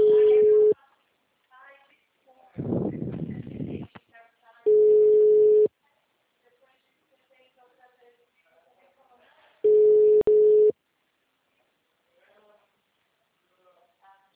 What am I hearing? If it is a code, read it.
Ringback tone (Europe, 425 Hz), with speech.
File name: out- -2023- - - .34